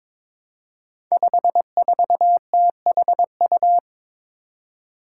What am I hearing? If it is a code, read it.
Morse code, heard as a keyed tone, about 22 wpm, 700 Hz. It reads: 54THU
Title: Five Challenging Letter-Number Combinations 22wpm